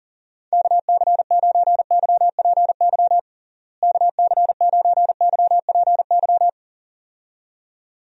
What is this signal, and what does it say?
Morse code, heard as a keyed tone, about 40 wpm, 700 Hz. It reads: KC9YPY KC9YPY
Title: USA Call Signs - 6 Characters - Repeated 2x - 40wpm